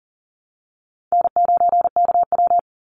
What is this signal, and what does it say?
Morse code, heard as a keyed tone, about 38 wpm, 700 Hz. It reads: N9KW
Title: NA SOTA Chaser Callsigns 40wpm